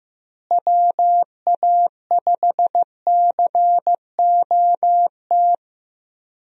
Morse code, heard as a keyed tone, about 15 wpm, 700 Hz. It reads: WA5COT
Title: USA Call Signs - 6 Characters - Repeated 3x - 15wpm